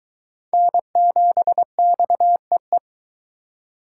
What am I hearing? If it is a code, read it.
Morse code, heard as a keyed tone, about 22 wpm, 700 Hz. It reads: N7XEE